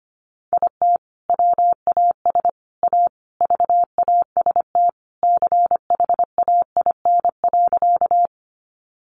Morse code, heard as a keyed tone, about 25 wpm, 700 Hz. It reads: IT WAS A VAST CHAIN.
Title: Sentences from Top 1500 Words 25wpm